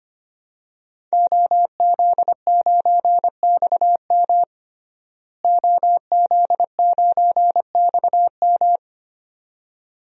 Morse code, heard as a keyed tone, about 25 wpm, 700 Hz. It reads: OZ9XM OZ9XM
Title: DXCC-100 Call Signs - 5 Characters - Repeated 2x - 25wpm